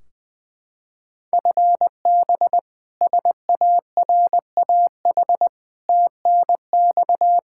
Morse code, heard as a keyed tone, about 20 wpm, 700 Hz. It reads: FB SARAH TNX
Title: K1USN SST QSO Practice 20wpm